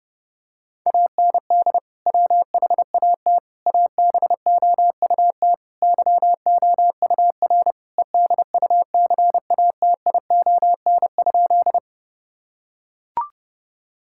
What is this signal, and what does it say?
Morse code, heard as a keyed tone, about 32 wpm, 700 Hz. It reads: AND WHAT ABOUT YOUR EDUCATION?
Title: Sentences from Top 1000 Words - Review Part 6 - 30wpm